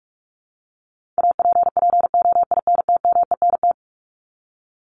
Morse code, heard as a keyed tone, about 44 wpm, 700 Hz. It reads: APPOINTMENT
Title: Top 1751-2000 Words 45wpm